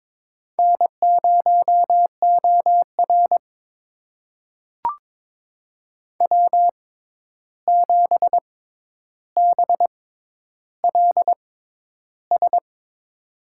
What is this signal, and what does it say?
Morse code, heard as a keyed tone, about 22 wpm, spaced as for 11 wpm, 700 Hz. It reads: N0OR W7BLS
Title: USA Call Signs - 6x Character Spacing - 22wpm